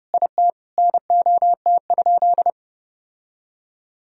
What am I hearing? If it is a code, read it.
Morse code, heard as a keyed tone, about 30 wpm, 700 Hz. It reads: IT NOT?
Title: Sentences from Top 1000 Words - Review Part 4 - 30wpm